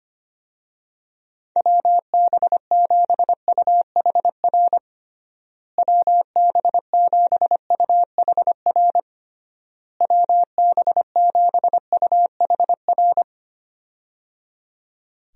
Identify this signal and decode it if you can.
Morse code, heard as a keyed tone, about 25 wpm, 700 Hz. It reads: WB7UHR WB7UHR WB7UHR
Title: USA Call Signs - 6 Characters - Repeated 3x - 25wpm